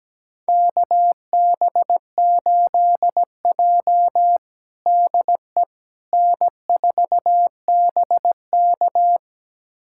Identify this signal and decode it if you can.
Morse code, heard as a keyed tone, about 17 wpm, 700 Hz. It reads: KB8J DE N4BK
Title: USA Callsign DE Callsign - 4 Characters - 17wpm